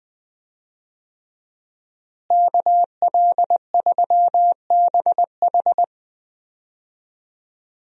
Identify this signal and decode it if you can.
Morse code, heard as a keyed tone, about 20 wpm, 700 Hz. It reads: KL3BH
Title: USA Call Signs - 5 Characters - Repeated 3x - 20wpm